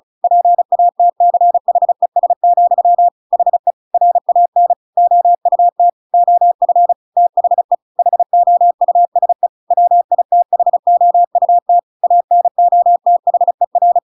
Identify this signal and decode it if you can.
Morse code, heard as a keyed tone, about 34 wpm, 700 Hz. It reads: PATCHES, HE RAN OUT OF THE HOUSE WITHOUT ANOTHER
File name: Pinocchio / Pinocchio ch8 35 35 ik1hge